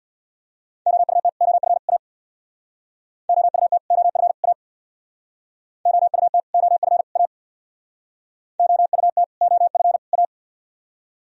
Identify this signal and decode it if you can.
Morse code, heard as a keyed tone, about 58 wpm, 700 Hz. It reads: OUT OF A OUT OF A OUT OF A OUT OF A
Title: Speed-Racing - Sets of 3 Words 50wpm